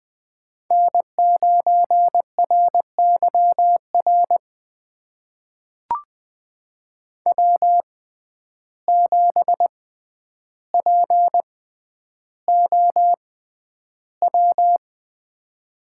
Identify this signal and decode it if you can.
Morse code, heard as a keyed tone, about 20 wpm, spaced as for 10 wpm, 700 Hz. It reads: N9RYR W7POW
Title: USA Call Signs - 5 Characters - 6x Character Spacing - 20wpm